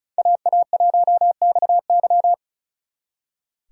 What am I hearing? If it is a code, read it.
Morse code, heard as a keyed tone, about 39 wpm, 700 Hz. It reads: AA1XY
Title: Speed Racing - USA 2x2 Call Signs 35wpm